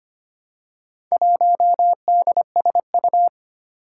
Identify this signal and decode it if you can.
Morse code, heard as a keyed tone, about 25 wpm, 700 Hz. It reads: 1DSU
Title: Four Challenging Letter-Number Combinations 25wpm